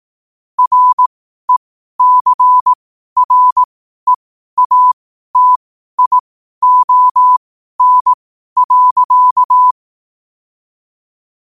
Morse code, heard as a keyed tone, about 18 wpm, spaced as for 12 wpm, 1000 Hz. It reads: RECREATION.
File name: qotdfiles / 12wpm